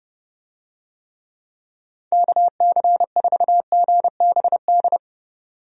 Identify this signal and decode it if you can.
Morse code, heard as a keyed tone, about 31 wpm, 700 Hz. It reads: KC4GBD
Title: USA Call Signs - 6 Characters - 30wpm